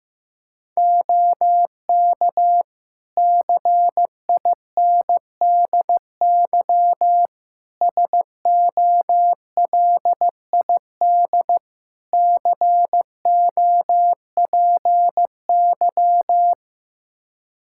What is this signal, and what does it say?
Morse code, heard as a keyed tone, about 15 wpm, 700 Hz. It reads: OK CINDY SOLID COPY
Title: K1USN SST QSO Practice 15wpm